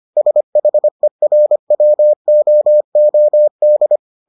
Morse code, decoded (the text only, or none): SHERWOOD